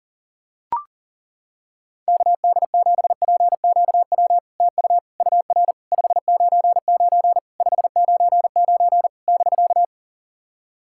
Morse code, heard as a keyed tone, about 40 wpm, 700 Hz. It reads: KD7PQW TU UR 599 599 <BK>
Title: SOTA QSO Practice 40wpm